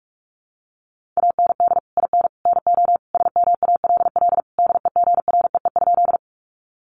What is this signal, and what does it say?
Morse code, heard as a keyed tone, about 44 wpm, 700 Hz. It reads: AND IN NO SMALL DEGREE?
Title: Sentences from Top 1500 Words 45wpm